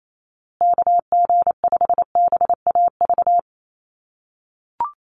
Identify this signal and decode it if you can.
Morse code, heard as a keyed tone, about 28 wpm, 700 Hz. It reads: KG5BAV
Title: USA Call Signs Repeated 2x 28wpm